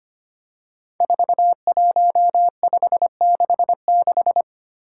Morse code, heard as a keyed tone, about 25 wpm, 700 Hz. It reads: 41566
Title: Five Challenging Letter-Number Combinations 25wpm